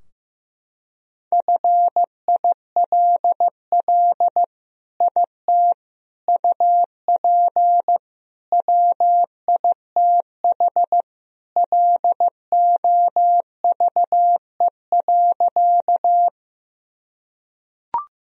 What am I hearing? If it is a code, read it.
Morse code, heard as a keyed tone, about 15 wpm, 700 Hz. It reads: FILL IT UP WITH LOVE.
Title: Sentences from Top 1000 Words - Review Part 3 - 15wpm